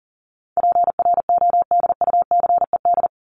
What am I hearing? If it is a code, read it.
Morse code, heard as a keyed tone, about 38 wpm, 700 Hz. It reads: PRODUCED
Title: ED Suffix - Top 100 Words with Base Words 40wpm